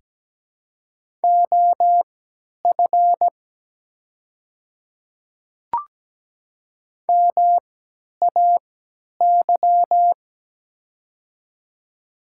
Morse code, heard as a keyed tone, about 17 wpm, spaced as for 10 wpm, 700 Hz. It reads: OF MAY
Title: 3x Character Spacing - Top 100 Words - 17wpm